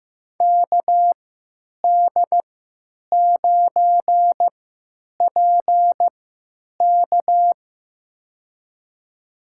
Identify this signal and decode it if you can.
Morse code, heard as a keyed tone, about 15 wpm, spaced as for 9 wpm, 700 Hz. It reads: KD9PK
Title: USA Call Signs - 5 Characters - 3x Character Spacing - 15wpm